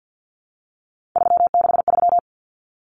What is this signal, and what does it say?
Morse code, heard as a keyed tone, about 50 wpm, 700 Hz. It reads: 363
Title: Common Cut Numbers - 3-Digit 50wpm